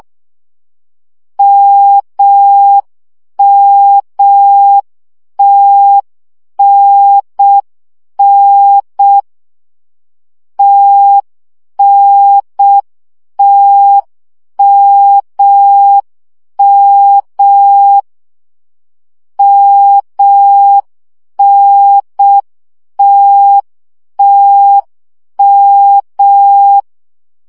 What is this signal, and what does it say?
Morse code, heard as a keyed tone, about 6 wpm, 800 Hz. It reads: MMTNN TNTMM MNTTM
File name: CW gyakorlas t n m 5 baud